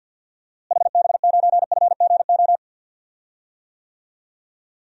Morse code, heard as a keyed tone, about 47 wpm, 700 Hz. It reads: HB9FGO
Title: DXCC-100 Call Signs - 6 Characters - Repeated 2x - 50wpm